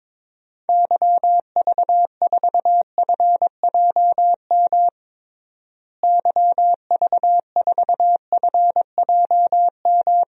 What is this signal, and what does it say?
Morse code, heard as a keyed tone, about 22 wpm, 700 Hz. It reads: YV4FJM YV4FJM
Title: DXCC-100 Call Signs - 6 Characters - Repeated 2x - 22wpm